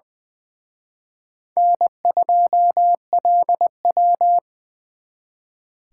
Morse code, heard as a keyed tone, about 24 wpm, 700 Hz. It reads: N2LW
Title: Speed Racing - USA 1x2 Call Signs 20wpm